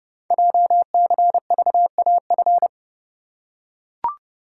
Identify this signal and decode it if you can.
Morse code, heard as a keyed tone, about 30 wpm, 700 Hz. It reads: JCVAF
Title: Code Groups - 5 Letters - 30wpm